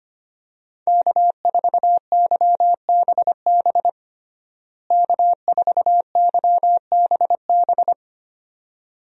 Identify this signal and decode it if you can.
Morse code, heard as a keyed tone, about 25 wpm, 700 Hz. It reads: K4YBB K4YBB
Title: USA Call Signs - 5 Characters - Repeated 2x - 25wpm